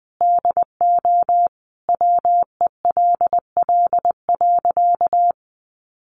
Morse code, heard as a keyed tone, about 20 wpm, 700 Hz. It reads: DO WELL.
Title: Sentences from Top 1400 Words 20wpm